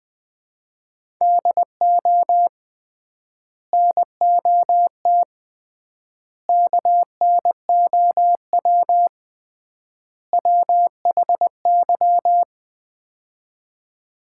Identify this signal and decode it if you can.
Morse code, heard as a keyed tone, about 20 wpm, 700 Hz. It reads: DO NOT KNOW WHY
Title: 3x Word-Spacing - Sets of 4 Words - 20wpm